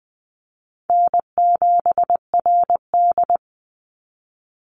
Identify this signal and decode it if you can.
Morse code, heard as a keyed tone, about 20 wpm, 700 Hz. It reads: N7RD
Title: Easy Call Signs Repeated 3x - 20wpm